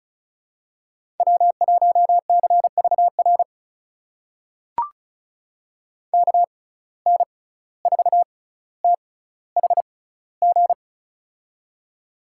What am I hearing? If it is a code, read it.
Morse code, heard as a keyed tone, about 36 wpm, spaced as for 17 wpm, 700 Hz. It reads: W1CVR KN4THG
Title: USA Call Signs - 6x Character Spacing - 35wpm